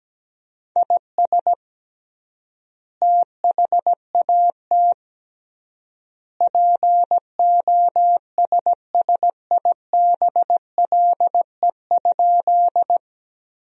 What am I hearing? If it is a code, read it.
Morse code, heard as a keyed tone, about 17 wpm, 700 Hz. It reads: IS THAT POSSIBLE?
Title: 3x Word-Spacing - Sentences from Top 500 Words - 17wpm